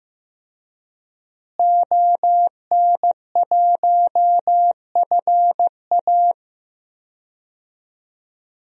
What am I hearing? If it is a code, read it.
Morse code, heard as a keyed tone, about 15 wpm, 700 Hz. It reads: ON1FA
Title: DXCC-100 Call Signs - 5 Characters - Repeated 2x - 15wpm